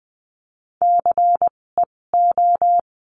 Morse code, heard as a keyed tone, about 20 wpm, spaced as for 16 wpm, 700 Hz. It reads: CEO
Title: 256 Lesson 42a - Words 12-20wpm